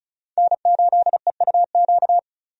Morse code, heard as a keyed tone, about 36 wpm, 700 Hz. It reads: N8EUQ
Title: USA Call Signs - 5 Characters - 35wpm